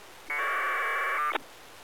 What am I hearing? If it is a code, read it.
Packet radio (AX.25, as in APRS).